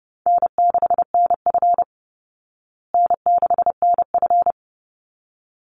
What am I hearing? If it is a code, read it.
Morse code, heard as a keyed tone, about 30 wpm, 700 Hz. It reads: N6NF N6NF
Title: Easy Call Signs Repeated 2x - 30wpm